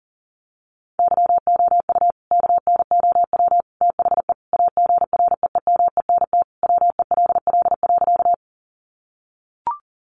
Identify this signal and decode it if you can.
Morse code, heard as a keyed tone, about 40 wpm, 700 Hz. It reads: YOU KNOW THE AGREEMENT WELL.
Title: Sentences from Top 1500 Words 40wpm